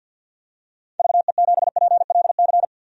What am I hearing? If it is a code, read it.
Morse code, heard as a keyed tone, about 51 wpm, 700 Hz. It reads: VE7PLC
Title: DXCC-100 Call Signs - 6 Characters - 50wpm